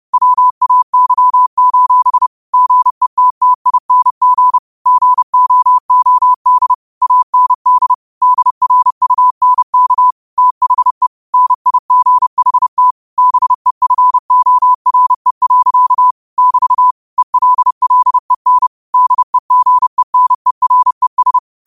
Morse code, heard as a keyed tone, about 30 wpm, 1000 Hz. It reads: WAY: GETTING GOOD AND DRUNK THE NIGHT BEFORE. <BT> ELLEN DEGENERES
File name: qotdfiles / 30wpm